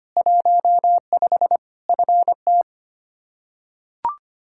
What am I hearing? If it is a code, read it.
Morse code, heard as a keyed tone, about 25 wpm, 700 Hz. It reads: 15 FT
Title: Practice QSO - Exchange 2 - Info - 25wpm